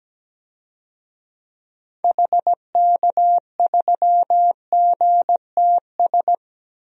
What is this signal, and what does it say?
Morse code, heard as a keyed tone, about 17 wpm, 700 Hz. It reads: HK3GTS
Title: DXCC-100 Call Signs - 6 Characters - Repeated 3x - 17wpm